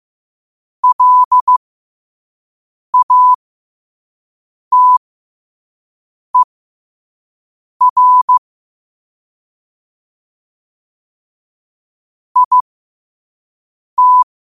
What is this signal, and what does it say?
Morse code, heard as a keyed tone, about 15 wpm, spaced as for 5 wpm, 1000 Hz. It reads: LATER IT
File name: qotdfiles / 5wpm